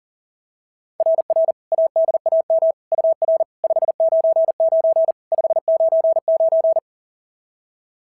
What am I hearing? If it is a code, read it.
Morse code, heard as a keyed tone, about 40 wpm, 690 Hz. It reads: RR ADAM UR 599 599
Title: SOTA QSO Practice 40wpm